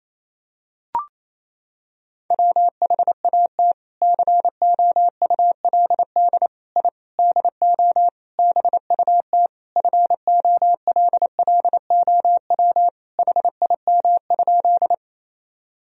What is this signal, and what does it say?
Morse code, heard as a keyed tone, about 28 wpm, 700 Hz. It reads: WHAT COULD I DO BUT FOLLOW HIM?
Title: Sentences from Top 1000 Words - Review Part 1 - 28wpm